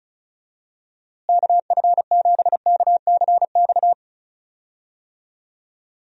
Morse code, heard as a keyed tone, about 34 wpm, 700 Hz. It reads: KF7KCX
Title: USA Call Signs - 6 Characters - Repeated 3x - 35wpm